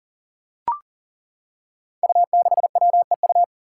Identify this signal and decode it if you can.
Morse code, heard as a keyed tone, about 40 wpm, 700 Hz. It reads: U6WEU
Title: Five Challenging Letter-Number Combinations 40wpm